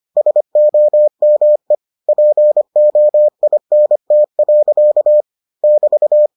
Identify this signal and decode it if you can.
Morse code, heard as a keyed tone, about 25 wpm, 600 Hz. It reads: SOME POINT. <BT>